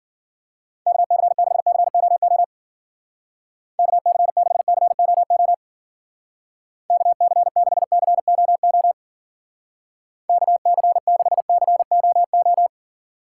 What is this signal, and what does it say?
Morse code, heard as a keyed tone, about 50 wpm, 700 Hz. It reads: KC6COO KC6COO KC6COO KC6COO
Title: Speed Racing - USA 2x3 Call Signs 40wpm